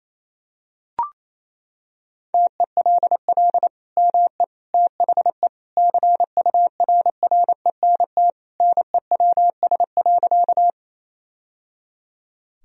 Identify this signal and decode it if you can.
Morse code, heard as a keyed tone, about 28 wpm, 700 Hz. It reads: TELL ME THE CURRENT NEWS.
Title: Sentences from Top 1000 Words - Review Part 4 - 28wpm